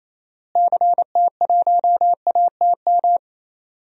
Morse code, heard as a keyed tone, about 27 wpm, 700 Hz. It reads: CT1ATM
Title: DXCC-100 Call Signs - 6 Characters - Repeated 2x - 28wpm